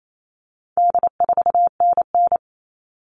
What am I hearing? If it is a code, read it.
Morse code, heard as a keyed tone, about 28 wpm, 700 Hz. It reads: D4NN
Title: All Cut Numbers - 4-Digit 28wpm